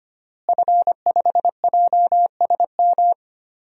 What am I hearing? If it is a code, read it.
Morse code, heard as a keyed tone, about 25 wpm, 700 Hz. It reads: F5JSM
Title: DXCC-100 Call Signs - 5 Characters - Repeated 3x - 25wpm